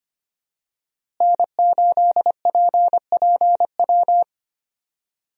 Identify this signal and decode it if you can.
Morse code, heard as a keyed tone, about 25 wpm, 700 Hz. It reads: N8PPW
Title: USA Call Signs - 5 Characters - Repeated 3x - 25wpm